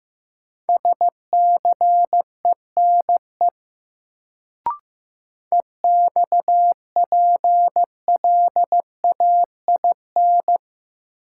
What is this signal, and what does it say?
Morse code, heard as a keyed tone, about 15 wpm, 700 Hz. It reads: SCENE EXPLAIN
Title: Sentences from Top 1100 Words - New Word Drill Alt - 15wpm